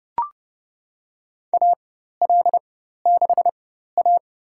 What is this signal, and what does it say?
Morse code, heard as a keyed tone, about 30 wpm, spaced as for 14 wpm, 700 Hz. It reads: AL6A
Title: USA Call Signs - 4x Character Spacing - 30wpm